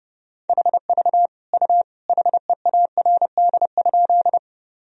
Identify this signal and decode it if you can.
Morse code, heard as a keyed tone, about 32 wpm, 700 Hz. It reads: HV U HEARD?
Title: Sentences from Top 400 Words and Abbreviations 30wpm